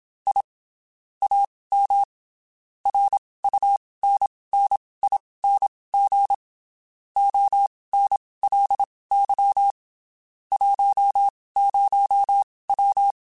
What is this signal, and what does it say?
Morse code, heard as a keyed tone, about 27 wpm, spaced as for 19 wpm, 800 Hz. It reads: I AM RUNNING ONLY 10W
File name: (spaced) / QSOs Spaced-mp3s / kc7sac @25 sp